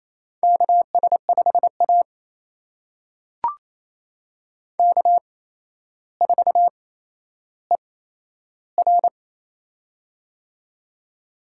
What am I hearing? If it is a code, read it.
Morse code, heard as a keyed tone, about 28 wpm, spaced as for 12 wpm, 700 Hz. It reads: KS5A K4ER
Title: USA Call Signs - 4 Characters - 8x Character Spacing - 28wpm